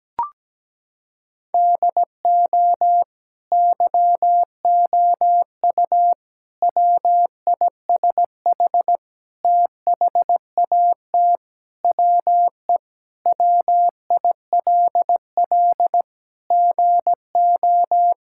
Morse code, heard as a keyed tone, about 17 wpm, 700 Hz. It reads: DO YOU WISH THAT WE WILL GO
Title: Sentences from Top 1600 Words 17wpm